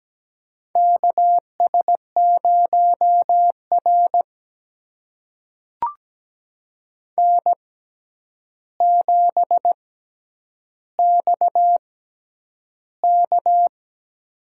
Morse code, heard as a keyed tone, about 17 wpm, spaced as for 9 wpm, 700 Hz. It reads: KS0R N7XK
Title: USA Call Signs - 6x Character Spacing - 17wpm